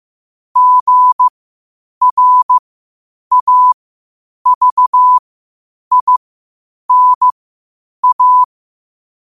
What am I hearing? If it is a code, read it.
Morse code, heard as a keyed tone, about 15 wpm, spaced as for 8 wpm, 1000 Hz. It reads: GRAVINA
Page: qotd-at-8-wpm